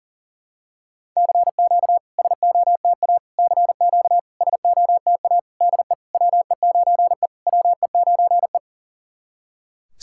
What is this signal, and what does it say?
Morse code, heard as a keyed tone, about 40 wpm, 700 Hz. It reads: CQ SOTA CQ SOTA DE WE9E WE9E
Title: SOTA QSO Practice 40wpm